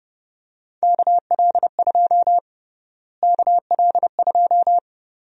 Morse code, heard as a keyed tone, about 31 wpm, 700 Hz. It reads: KL2 KL2
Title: USA Call Sign Prefixes - Repeated 2x - 30wpm